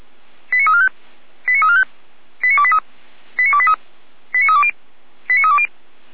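Five-tone selective call (ZVEI), heard as quick runs of five tones, tones of about 70 ms. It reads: 89345 89345 89283 89283 89320 89320
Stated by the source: ZVEI five-tone fire brigade call 89319 received on an SDR, logged by Feuerwehr Orlamünde.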